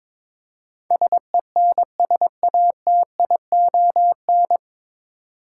Morse code, heard as a keyed tone, about 22 wpm, 700 Hz. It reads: SENSATION
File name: 9 Letter Words - First 3 Letters Given - 22wpm